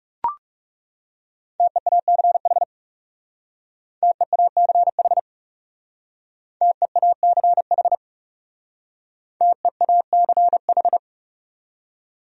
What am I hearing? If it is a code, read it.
Morse code, heard as a keyed tone, about 36 wpm, 700 Hz. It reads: TEACH TEACH TEACH TEACH
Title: Speed-Racing Top 500 Words 30wpm